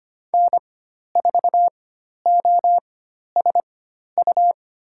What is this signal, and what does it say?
Morse code, heard as a keyed tone, about 25 wpm, spaced as for 12 wpm, 700 Hz. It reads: N4OSU
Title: USA Call Signs - 5 Characters - 4x Character Spacing - 25wpm